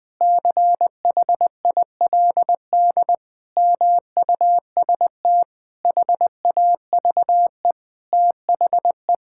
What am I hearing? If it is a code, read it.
Morse code, heard as a keyed tone, about 21 wpm, 700 Hz. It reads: CHILD MUST HAVE THE
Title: Sentences from Top 1000 Words - Review Part 6 - 20wpm